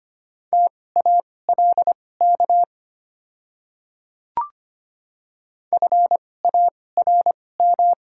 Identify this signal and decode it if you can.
Morse code, heard as a keyed tone, about 25 wpm, spaced as for 18 wpm, 700 Hz. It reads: TALK FARM
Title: 2x Character Spacing - Top 500 Words - 25wpm